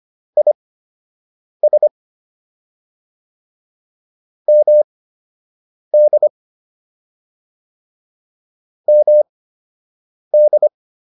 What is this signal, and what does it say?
Morse code, heard as a keyed tone, about 25 wpm, spaced as for 7 wpm, 600 Hz. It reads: IS MD MD